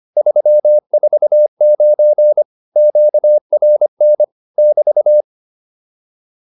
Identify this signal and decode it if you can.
Morse code, heard as a keyed tone, about 25 wpm, 600 Hz. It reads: 349 QRN <BT>